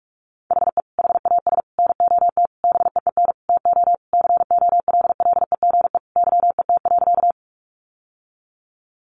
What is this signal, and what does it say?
Morse code, heard as a keyed tone, about 46 wpm, 700 Hz. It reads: HE HAS NOT BEEN TO COLLEGE YET.
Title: Sentences from Top 1300 Words 45wpm